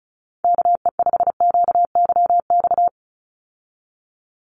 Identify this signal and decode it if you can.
Morse code, heard as a keyed tone, about 34 wpm, 700 Hz. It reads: KE5QYX
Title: USA Call Signs 35wpm